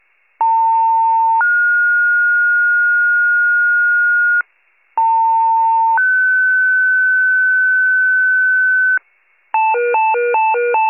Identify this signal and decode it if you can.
Two-tone page, heard as a one-second tone, then a three-second tone, about 900 then 1430 Hz, about 900 then 1500 Hz. It is heard from 0.4 to 9.0 s.